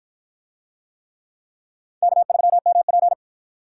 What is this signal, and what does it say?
Morse code, heard as a keyed tone, about 53 wpm, 700 Hz. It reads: K3MP